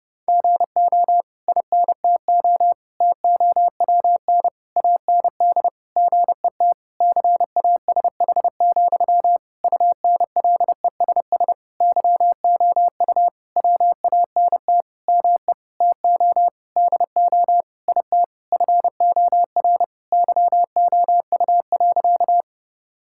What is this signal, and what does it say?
Morse code, heard as a keyed tone, about 30 wpm, 700 Hz. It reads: GO INTO TOWN AND GET CASH, UNLESS YOU WANT ME TO DO IT FOR YOU.
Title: Sentences from Top 1000 Words - Review Part 6 - 30wpm